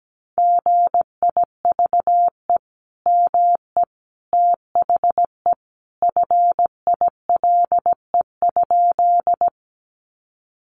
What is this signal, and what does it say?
Morse code, heard as a keyed tone, about 17 wpm, 700 Hz. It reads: GIVE ME THE FILE?
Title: Sentences from Top 1200 Words 17wpm